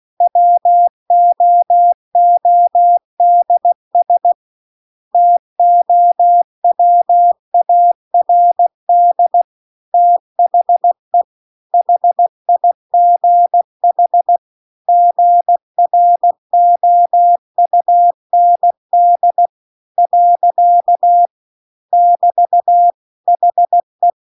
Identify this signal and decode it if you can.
Morse code, heard as a keyed tone, about 16 wpm, 700 Hz. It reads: WOODS TOWARD THE HIGH GROUND . <BT> HE
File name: CW-The-Big-Twoheaded-River-2-16WPM